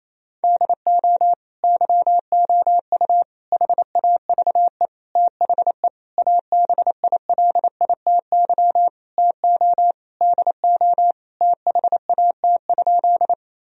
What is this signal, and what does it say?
Morse code, heard as a keyed tone, about 28 wpm, 700 Hz. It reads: DO YOU HAVE THE ABILITY TO DO THAT?
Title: Sentences from Top 1000 Words - Review Part 3 - 28wpm